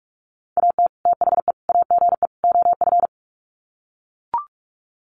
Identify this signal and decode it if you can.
Morse code, heard as a keyed tone, about 44 wpm, 700 Hz. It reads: AT THE AGE OF
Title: Sets of 4 Words 45wpm